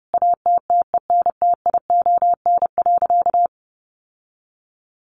Morse code, heard as a keyed tone, about 30 wpm, 700 Hz. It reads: ATTENTION.
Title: Sentences from Top 1300 Words 30wpm